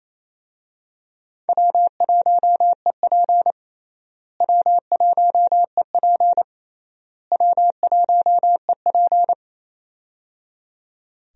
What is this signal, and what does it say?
Morse code, heard as a keyed tone, about 28 wpm, 700 Hz. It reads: W1EP W1EP W1EP
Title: USA Call Signs - 4 Characters - Repeated 3x - 28wpm